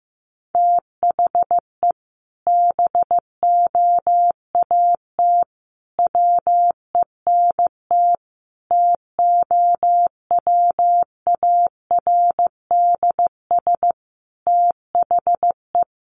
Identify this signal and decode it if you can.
Morse code, heard as a keyed tone, about 15 wpm, 700 Hz. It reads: THE BOAT WENT TOWARDS THE
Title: Sentences from Top 1100 Words 15wpm